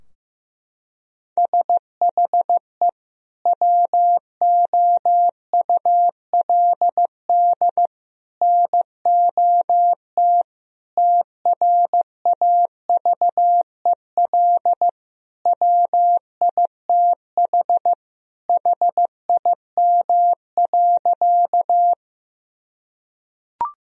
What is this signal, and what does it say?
Morse code, heard as a keyed tone, about 15 wpm, 700 Hz. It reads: SHE WOULD NOT TRAVEL WITH HIM.
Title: Sentences from Top 1000 Words - Review Part 2 - 15wpm